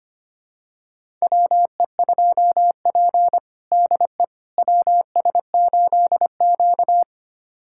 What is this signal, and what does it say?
Morse code, heard as a keyed tone, about 25 wpm, 700 Hz. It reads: WE2P DE WS8Q
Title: USA Callsign DE Callsign - 4 Characters - 25wpm